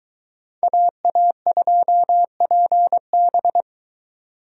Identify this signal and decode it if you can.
Morse code, heard as a keyed tone, about 22 wpm, 700 Hz. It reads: AA2PB